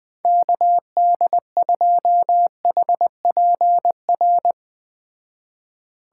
Morse code, heard as a keyed tone, about 21 wpm, 700 Hz. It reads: KD2HPR